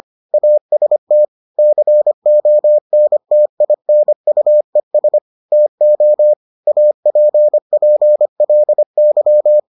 Morse code, heard as a keyed tone, about 25 wpm, 600 Hz. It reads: AST CONTINUES TO APPLY